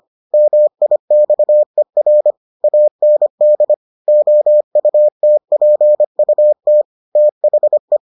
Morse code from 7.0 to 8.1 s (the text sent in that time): THE